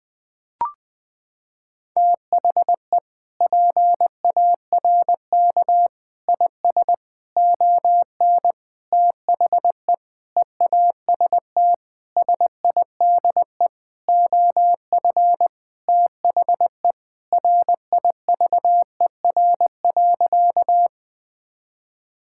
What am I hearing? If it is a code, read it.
Morse code, heard as a keyed tone, about 20 wpm, 700 Hz. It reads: THE PARK IS ON THE EAST SIDE OF THE RIVER.
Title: Sentences with 6-Word Phrases - 20wpm